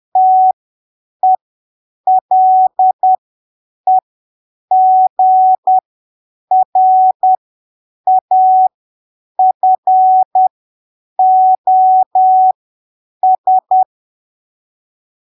Morse code, heard as a keyed tone, about 10 wpm, spaced as for 7 wpm, 750 Hz. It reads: TELEGRAFOS
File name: arquivos / texto / texto 14 10ppm